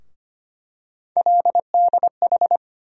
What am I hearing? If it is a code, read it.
Morse code, heard as a keyed tone, about 25 wpm, 700 Hz. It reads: LDH
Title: USA Call Sign Postfixes - 3 Letters 25wpm